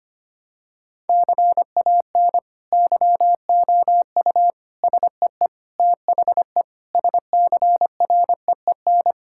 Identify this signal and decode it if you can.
Morse code, heard as a keyed tone, about 25 wpm, 700 Hz. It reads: CAN YOU SEE THE SCREEN
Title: Sentences from Top 1000 Words - Review Part 6 - 25wpm